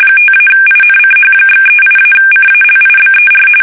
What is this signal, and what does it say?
RTTY, about 45.45 baud. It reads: 5-INCH/38-CALIBER
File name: fsk850 / battleshipnj